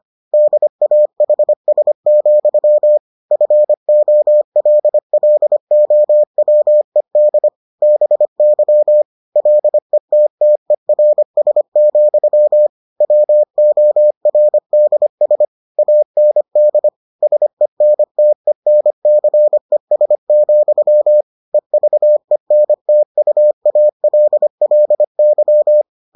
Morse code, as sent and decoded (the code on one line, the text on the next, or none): -.. .- .... ... --..-- / ..-. --- .-.. .-.. --- .-- . -.. / -... -.-- / .-.. - - . .-. ... --..-- / .-- --- .-. -.. ... / .- -. -.. / ... -. - . -. -.-. ... --..-- / . ...- . -. - ..- .- .-.. .-.. -.--
DAHS, FOLLOWED BY LETTERS, WORDS AND SENTENCES, EVENTUALLY